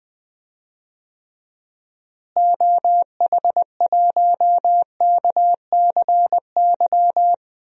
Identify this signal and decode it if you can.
Morse code, heard as a keyed tone, about 21 wpm, 700 Hz. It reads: OH1KCY
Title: DXCC-100 Call Signs - 6 Characters - Repeated 3x - 20wpm